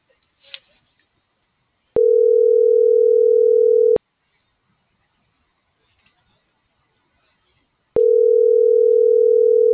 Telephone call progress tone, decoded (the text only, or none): ringback tone